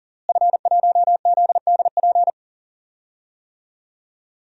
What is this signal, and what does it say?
Morse code, heard as a keyed tone, about 40 wpm, 700 Hz. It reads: F1ZDP
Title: DXCC-100 Call Signs - 5 Characters - Repeated 3x - 40wpm